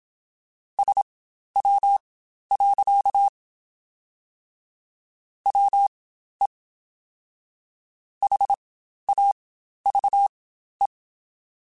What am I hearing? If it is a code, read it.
Morse code, heard as a keyed tone, about 26 wpm, spaced as for 12 wpm, 800 Hz. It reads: SW. WE HAVE
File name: Spaced Characters / Spaced Characters-mp3s / Weather5 5 sp @25